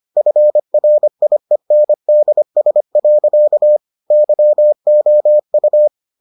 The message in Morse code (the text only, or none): FRIENDS. YOU